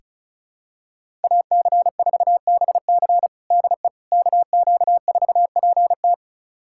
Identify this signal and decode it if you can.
Morse code, heard as a keyed tone, about 34 wpm, 700 Hz. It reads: AC4BC DE KQ4PT
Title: USA Callsign DE Callsign - 5 Characters - 35wpm